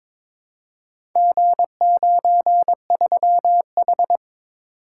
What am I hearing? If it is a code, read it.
Morse code, heard as a keyed tone, about 22 wpm, 700 Hz. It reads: G93H
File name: practice / 22wpm / Four Letter-Number Combinations - 22wpm